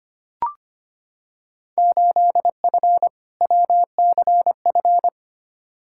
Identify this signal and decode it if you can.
Morse code, heard as a keyed tone, about 25 wpm, 700 Hz. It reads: 8F WCF
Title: Field Day Exchanges Encore - 25wpm